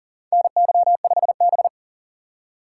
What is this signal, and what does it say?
Morse code, heard as a keyed tone, about 40 wpm, 700 Hz. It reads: NY5B
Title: USA Call Signs - 4x Character Spacing - 40wpm